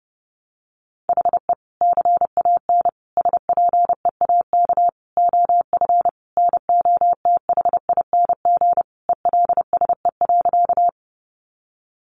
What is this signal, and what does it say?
Morse code, heard as a keyed tone, about 30 wpm, 700 Hz. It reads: HE CAN SPEAK OF NOTHING ELSE.
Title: Sentences from Top 1100 Words 30wpm